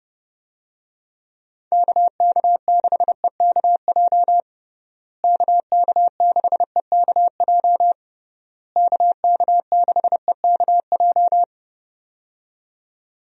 Morse code, heard as a keyed tone, about 30 wpm, 700 Hz. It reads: KK6EKJ KK6EKJ KK6EKJ
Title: USA Call Signs - 6 Characters - Repeated 3x - 30wpm